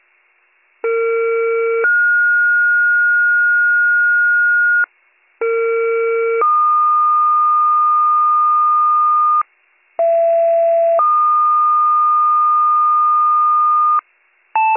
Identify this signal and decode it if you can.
Two-tone page, heard as a one-second tone, then a three-second tone, about 460 then 1430 Hz, about 460 then 1220 Hz, about 670 then 1220 Hz. It is heard from 0.8 to 14.0 s.